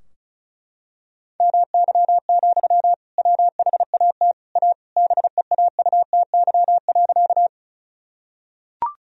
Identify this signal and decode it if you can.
Morse code, heard as a keyed tone, about 34 wpm, 700 Hz. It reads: MY, WHAT A BEAUTY.
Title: Sentences from Top 1000 Words - Review Part 2 - 35wpm